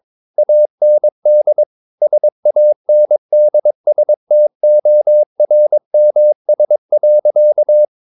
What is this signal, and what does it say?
Morse code, heard as a keyed tone, about 22 wpm, 600 Hz. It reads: AND SANDSTORMS.
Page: Romanen Never af Ken Follet på engelsk.